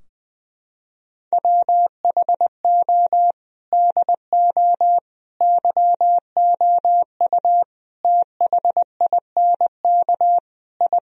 Morse code, heard as a keyed tone, about 19 wpm, 700 Hz. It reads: WHO DO YOU THINK I
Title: Sentences from Top 1000 Words - Review Part 4 - 20wpm